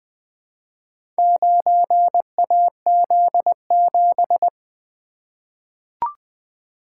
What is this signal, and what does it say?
Morse code, heard as a keyed tone, about 21 wpm, 700 Hz. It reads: 9AZ7
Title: Four Letter-Number Combinations - 20wpm